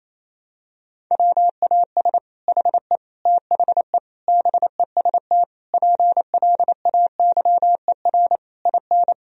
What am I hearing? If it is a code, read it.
Morse code, heard as a keyed tone, about 28 wpm, 700 Hz. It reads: WAS HE THE BEST PLAYER IN
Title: Sentences from Top 1000 Words - Review Part 4 - 28wpm